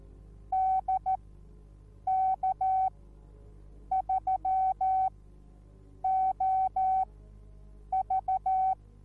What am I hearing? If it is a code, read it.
Morse code, heard as a keyed tone, about 14 wpm, spaced as for 7 wpm, 760 Hz. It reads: DK3OV